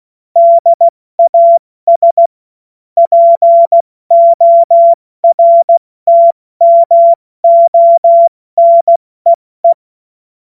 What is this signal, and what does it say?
Morse code, heard as a keyed tone, about 16 wpm, spaced as for 14 wpm, 680 Hz. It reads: DAS PORTMONEE
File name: Morsekurs / Witze / Witz 22. Februar 2022 70 F80